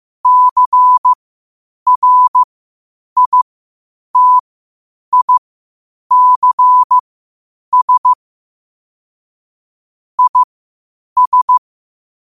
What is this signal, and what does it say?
Morse code, heard as a keyed tone, about 15 wpm, spaced as for 8 wpm, 1000 Hz. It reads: CRITICS IS
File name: qotdfiles / 8wpm